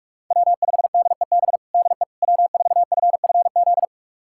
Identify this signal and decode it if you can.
Morse code, heard as a keyed tone, about 46 wpm, 700 Hz. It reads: W5DEB DE W4FFZ
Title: USA Callsign DE Callsign - 5 Characters - 45wpm